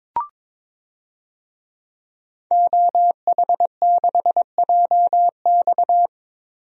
Morse code, heard as a keyed tone, about 22 wpm, 700 Hz. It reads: OH6JX
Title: DXCC-100 Call Signs - 5 Characters - 22wpm